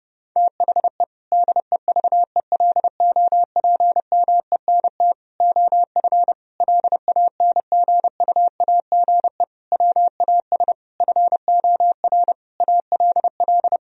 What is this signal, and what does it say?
Morse code, heard as a keyed tone, about 30 wpm, 700 Hz. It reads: THE DEVELOPMENT OF LANGUAGE WAS FOR ALL
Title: Sentences from Top 1000 Words - Review Part 4 - 30wpm